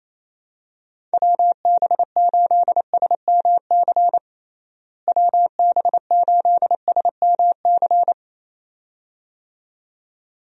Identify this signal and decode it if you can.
Morse code, heard as a keyed tone, about 28 wpm, 700 Hz. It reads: WB8SMC WB8SMC
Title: USA Call Signs - 6 Characters - Repeated 2x - 28wpm